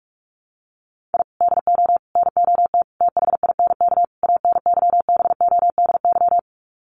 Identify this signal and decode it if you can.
Morse code, heard as a keyed tone, about 44 wpm, 700 Hz. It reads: I DO NOT THINK ANYBODY
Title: Sets of 5 Words 45wpm